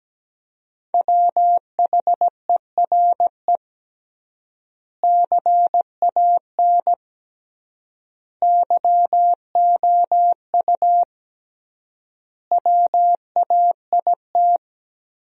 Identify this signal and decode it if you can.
Morse code, heard as a keyed tone, about 17 wpm, 700 Hz. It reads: WHERE CAN YOU WAIT
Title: 3x Word-Spacing - Sentences from Top 500 Words - 17wpm